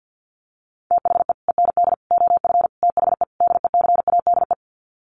Morse code, heard as a keyed tone, about 49 wpm, 700 Hz. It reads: THE END OF THE DECADE
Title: Sets of 5 Words Encore 50wpm